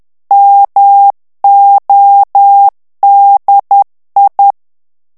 Morse code, heard as a keyed tone, about 10 wpm, 800 Hz. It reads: MODI
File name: morse mp3 / ITALIEN texte clair 1 vitesse 10 mots minute